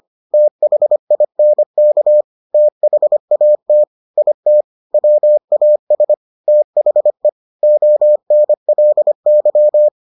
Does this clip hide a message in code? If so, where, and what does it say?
Yes: Morse code from 0.3 to 9.9 s, heard as a keyed tone, about 25 wpm, 600 Hz. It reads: THINK THAT IT WAS THE ONLY